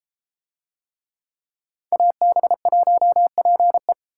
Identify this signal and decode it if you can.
Morse code, heard as a keyed tone, about 31 wpm, 700 Hz. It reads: AB1PE